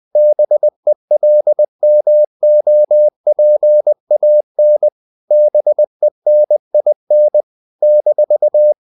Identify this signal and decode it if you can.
Morse code, heard as a keyed tone, about 20 wpm, 600 Hz. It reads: BELMOPAN BENIN -
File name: CW / Practice / NationalCapitals20